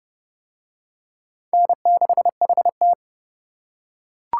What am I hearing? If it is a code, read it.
Morse code, heard as a keyed tone, about 27 wpm, 700 Hz. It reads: N6HT